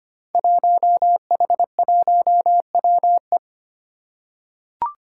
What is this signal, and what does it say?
Morse code, heard as a keyed tone, about 25 wpm, 700 Hz. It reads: 1H1WE
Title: Five Challenging Letter-Number Combinations 25wpm